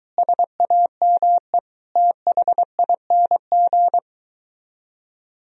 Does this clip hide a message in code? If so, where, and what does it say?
Yes: Morse code from 0.2 to 4.0 s, heard as a keyed tone, about 23 wpm, 700 Hz. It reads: SAME THING